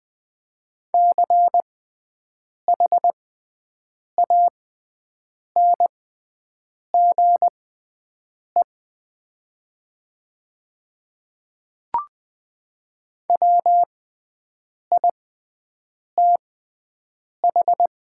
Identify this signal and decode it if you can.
Morse code, heard as a keyed tone, about 20 wpm, spaced as for 7 wpm, 700 Hz. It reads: CHANGE WITH
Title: 6x Character Spacing - Top 200 Words - 20wpm